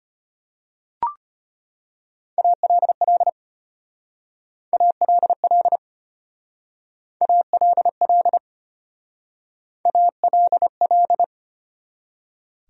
Morse code, heard as a keyed tone, about 33 wpm, 700 Hz. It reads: ALL ALL ALL ALL